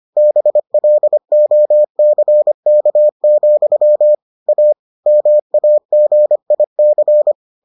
Morse code, heard as a keyed tone, about 25 wpm, 600 Hz. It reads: BLOCK, A MAGIC